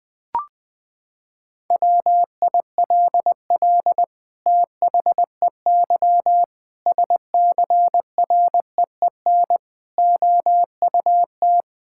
Morse code, heard as a keyed tone, about 21 wpm, 700 Hz. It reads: WILL THEY SCREEN OUT
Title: Sentences from Top 1000 Words - Review Part 6 - 20wpm